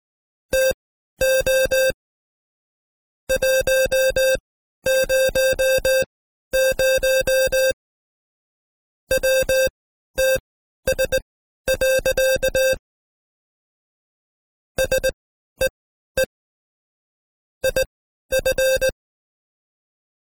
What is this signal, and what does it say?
Morse code, heard as a keyed tone, about 18 wpm, spaced as for 12 wpm, 530 Hz. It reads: TO 100 WTS. SEE IF
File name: Code Quick / AudioDownld2 / Seg132-Stepper5 11WPM